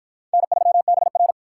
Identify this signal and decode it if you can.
Morse code, heard as a keyed tone, about 53 wpm, 700 Hz. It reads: N3BR